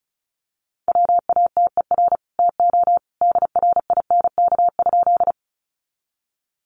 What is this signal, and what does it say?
Morse code, heard as a keyed tone, about 34 wpm, 700 Hz. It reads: WATER TO DRINK?
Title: Sentences from Top 1500 Words 35wpm